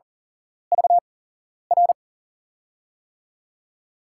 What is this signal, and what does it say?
Morse code, heard as a keyed tone, about 40 wpm, spaced as for 11 wpm, 700 Hz. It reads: VR
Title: USA Call Signs - 6 Characters - 8x Character Spacing - 40wpm